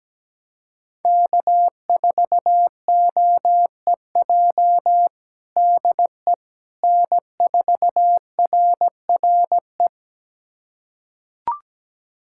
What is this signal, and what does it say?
Morse code, heard as a keyed tone, about 17 wpm, 700 Hz. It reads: K4OEJ DE N4RRE
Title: USA Callsign DE Callsign - 5 Characters - 17wpm